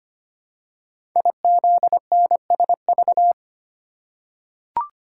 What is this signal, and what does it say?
Morse code, heard as a keyed tone, about 25 wpm, 700 Hz. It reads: IZNSV
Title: Code Groups - 5 Letters - 25wpm